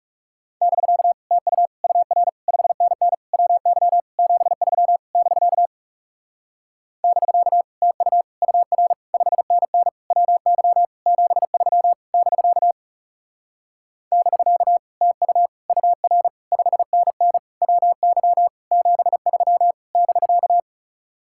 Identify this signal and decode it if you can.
Morse code, heard as a keyed tone, about 40 wpm, 700 Hz. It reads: <BK> TU UR 5NN WY 73 <BK> <BK> TU UR 5NN WY 73 <BK> <BK> TU UR 5NN WY 73 <BK>